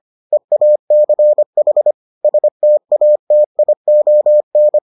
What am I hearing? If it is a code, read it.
Morse code, heard as a keyed tone, about 25 wpm, 600 Hz. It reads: EACH STATION